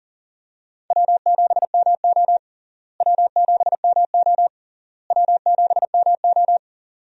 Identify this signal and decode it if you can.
Morse code, heard as a keyed tone, about 40 wpm, 700 Hz. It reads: W7MO W7MO W7MO
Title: USA Call Signs - 4 Characters - Repeated 3x - 40wpm